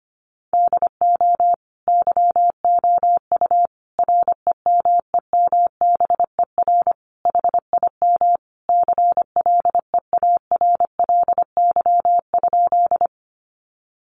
Morse code, heard as a keyed tone, about 25 wpm, 700 Hz. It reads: DO YOU REMEMBER HIM CLEARLY?
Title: Sentences from Top 1300 Words 25wpm